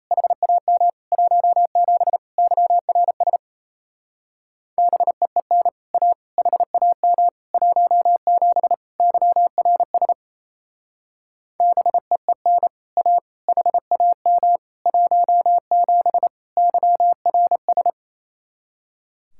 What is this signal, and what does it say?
Morse code, heard as a keyed tone, about 34 wpm, 700 Hz. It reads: HAM 17 YRS BEEN A HAM 17 YRS BEEN A HAM 17 YRS
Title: Speed-Racing Common QSO Phrases - 28wpm